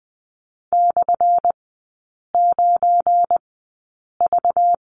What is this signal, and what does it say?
Morse code, heard as a keyed tone, about 20 wpm, spaced as for 8 wpm, 700 Hz. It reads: /9V
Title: 191 Lesson 31e - Letter v - Three Characters 20wpm